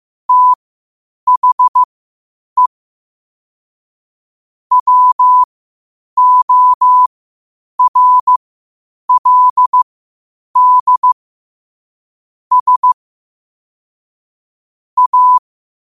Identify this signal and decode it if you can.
Morse code, heard as a keyed tone, about 15 wpm, spaced as for 8 wpm, 1000 Hz. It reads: THE WORLD S A
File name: qotdfiles / 8wpm